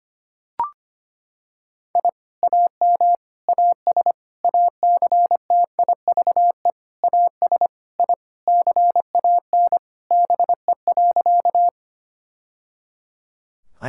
Morse code, heard as a keyed tone, about 25 wpm, 700 Hz. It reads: I AM AS ACTIVE AS I CAN BE.
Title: Sentences from Top 1000 Words - Review Part 6 - 25wpm